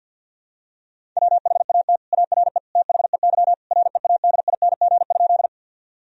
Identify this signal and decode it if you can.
Morse code, heard as a keyed tone, about 51 wpm, 700 Hz. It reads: WHAT ARE THEY READING?
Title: Sentences from Top 1000 Words - Review Part 3 - 50wpm